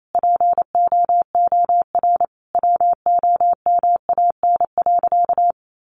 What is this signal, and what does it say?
Morse code, heard as a keyed tone, about 28 wpm, 700 Hz. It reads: POOR WOMAN.
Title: Sentences from Top 1100 Words 28wpm